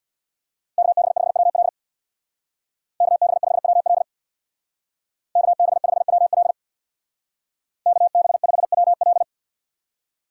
Morse code, heard as a keyed tone, about 64 wpm, 700 Hz. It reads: KB5WL KB5WL KB5WL KB5WL